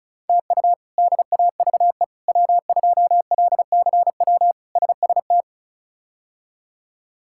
Morse code, heard as a keyed tone, about 36 wpm, 700 Hz. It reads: TU DAVE W2LCW SST
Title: K1USN SST QSO Practice - 35wpm